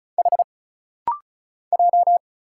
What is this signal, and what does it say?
Morse code, heard as a keyed tone, about 36 wpm, 700 Hz. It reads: H J
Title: Single Letters - Rapid-Fire 35wpm